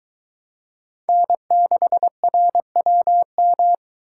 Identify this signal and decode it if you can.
Morse code, heard as a keyed tone, about 22 wpm, 700 Hz. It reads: N6RWM